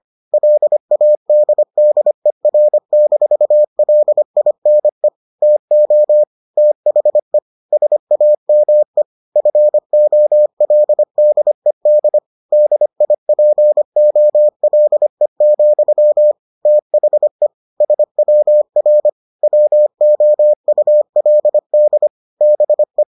Morse code, heard as a keyed tone, about 25 wpm, 600 Hz. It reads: LADDER-LINE TO THE SAME FOLDED DIPOLE, THE SWR WOULD BE